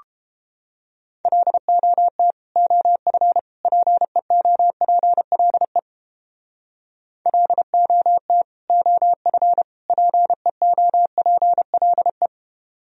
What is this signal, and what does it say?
Morse code, heard as a keyed tone, about 29 wpm, 700 Hz. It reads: LOT OF PEOPLE LOT OF PEOPLE